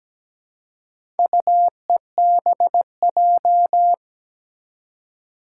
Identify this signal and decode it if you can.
Morse code, heard as a keyed tone, about 17 wpm, 700 Hz. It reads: UEBJ
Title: Four Challenging Letter-Number Combinations 17wpm